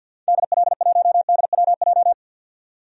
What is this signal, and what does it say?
Morse code, heard as a keyed tone, about 51 wpm, 700 Hz. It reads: DL1DWJ
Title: DXCC-100 Call Signs - 6 Characters - 50wpm